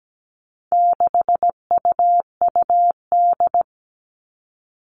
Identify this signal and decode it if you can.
Morse code, heard as a keyed tone, about 17 wpm, 700 Hz. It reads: 6UUD
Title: All Cut Numbers - 4-Digit 17wpm